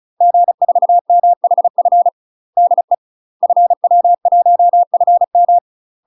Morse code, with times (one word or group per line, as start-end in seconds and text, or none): 0.2-2.1 G4MHF
2.6-2.9 DE
3.4-5.6 FW1FM